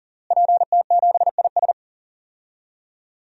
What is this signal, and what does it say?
Morse code, heard as a keyed tone, about 40 wpm, 700 Hz. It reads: PT7IS
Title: DXCC-100 Call Signs - 5 Characters - Repeated 2x - 40wpm